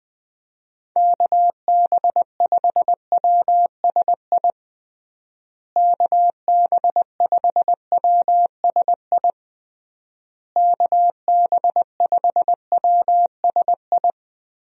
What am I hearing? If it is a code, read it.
Morse code, heard as a keyed tone, about 20 wpm, 700 Hz. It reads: KB5WSI KB5WSI KB5WSI
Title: USA Call Signs - 6 Characters - Repeated 3x - 20wpm